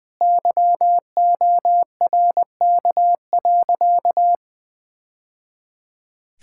Morse code, heard as a keyed tone, about 20 wpm, 700 Hz. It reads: YORK.
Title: Sentences with 3-Word Phrases - 20wpm